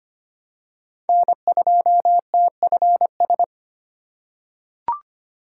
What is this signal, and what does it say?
Morse code, heard as a keyed tone, about 25 wpm, 700 Hz. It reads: N2TFS
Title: USA Call Signs - 5 Characters - 4x Character Spacing - 25wpm